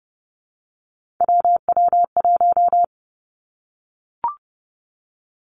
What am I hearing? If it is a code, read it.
Morse code, heard as a keyed tone, about 30 wpm, 700 Hz. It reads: WW1
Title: Three Challenging Letter-Number Combinations 30wpm